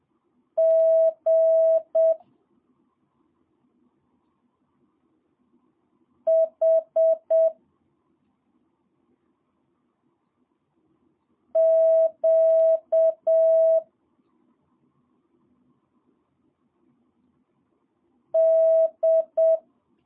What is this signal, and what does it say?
Morse code, heard as a keyed tone, about 7 wpm, 650 Hz. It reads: G H Q D
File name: jh8cbh / 2022cw